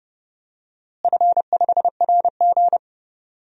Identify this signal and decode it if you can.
Morse code, heard as a keyed tone, about 30 wpm, 700 Hz. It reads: F5RG
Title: DXCC-100 Call Signs - 4 Characters - Repeated 3x - 30wpm